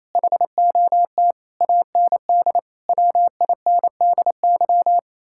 Morse code, heard as a keyed tone, about 32 wpm, 700 Hz. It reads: HOT AND WINDY
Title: Speed-Racing Common QSO Phrases - 28wpm